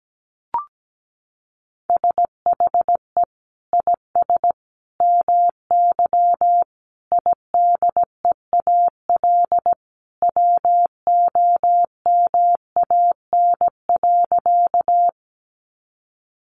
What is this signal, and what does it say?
Morse code, heard as a keyed tone, about 17 wpm, 700 Hz. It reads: SHE IS MY IDEAL WOMAN.
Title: Sentences from Top 1400 Words 17wpm